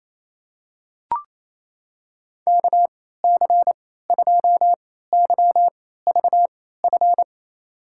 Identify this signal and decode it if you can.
Morse code, heard as a keyed tone, about 27 wpm, spaced as for 16 wpm, 700 Hz. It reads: KC2YVF
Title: USA Call Signs - 6 Characters - 3x Character Spacing - 28wpm